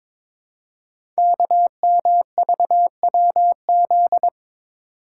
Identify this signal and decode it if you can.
Morse code, heard as a keyed tone, about 23 wpm, 700 Hz. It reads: KMVWZ
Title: Code Groups - 5 Letters - 22wpm